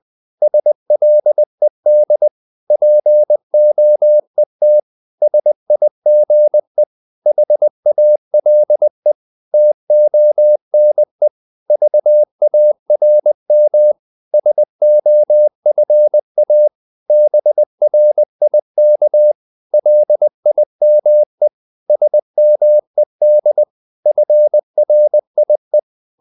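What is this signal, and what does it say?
Morse code, heard as a keyed tone, about 20 wpm, 600 Hz. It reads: SLED POET SIGE HALE TONE VARM SOFA BRIK LIME SMED FRIE
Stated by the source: Morse code practice of short Danish words at 20 wpm, from CW med Gnister.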